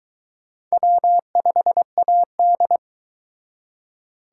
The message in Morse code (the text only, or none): W5AD